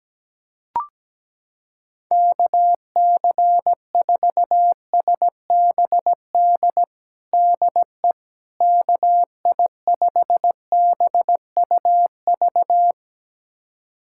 Morse code, heard as a keyed tone, about 17 wpm, 700 Hz. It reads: KC4SBD DE KI5BUV
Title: USA Callsign DE Callsign - 6 Characters - 17wpm